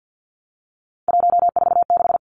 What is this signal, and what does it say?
Morse code, heard as a keyed tone, about 50 wpm, 700 Hz. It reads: 146
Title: Numbers - 3-Digit 50wpm